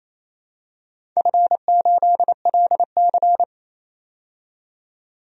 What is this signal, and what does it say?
Morse code, heard as a keyed tone, about 28 wpm, 700 Hz. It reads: F8LC
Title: DXCC-100 Call Signs - 4 Characters - 28wpm